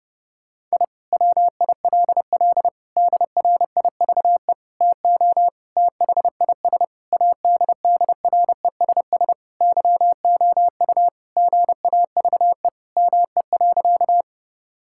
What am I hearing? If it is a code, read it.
Morse code, heard as a keyed tone, about 30 wpm, 700 Hz. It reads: I WILL DRIVE TO THIS ADDRESS YOU GAVE ME.
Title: Sentences from Top 1000 Words - Review Part 6 - 30wpm